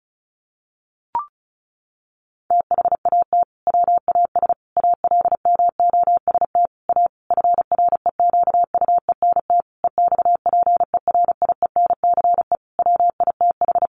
Morse code, heard as a keyed tone, about 34 wpm, 700 Hz. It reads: THAT WAS ALMOST A FREQUENT EXPERIENCE WITH
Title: Sentences from Top 1400 Words 35wpm